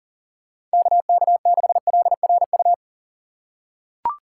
Morse code, heard as a keyed tone, about 39 wpm, 700 Hz. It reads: KK6LRU
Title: USA Call Signs - 6 Characters - 4x Character Spacing - 40wpm